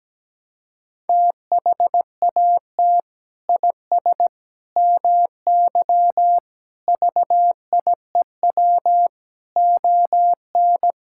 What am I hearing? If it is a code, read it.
Morse code, heard as a keyed tone, about 17 wpm, 700 Hz. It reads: THAT IS MY VIEW ON
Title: Sentences from Top 1000 Words - Review Part 6 - 17wpm